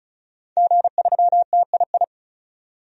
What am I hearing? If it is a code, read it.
Morse code, heard as a keyed tone, about 34 wpm, 700 Hz. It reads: G3TII
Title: DXCC-100 Call Signs - 5 Characters - 35wpm